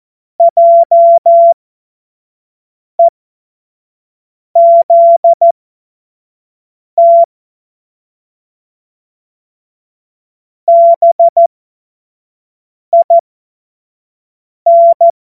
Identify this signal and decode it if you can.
Morse code, heard as a keyed tone, about 14 wpm, spaced as for 5 wpm, 680 Hz. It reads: JEZT BIN
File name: Morsekurs / Witze / Witz 11. Mai 2021, 25 F70